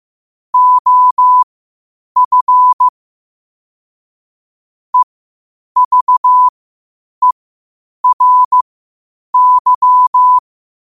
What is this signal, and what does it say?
Morse code, heard as a keyed tone, about 15 wpm, spaced as for 8 wpm, 1000 Hz. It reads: OF EVERY
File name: qotdfiles / 8wpm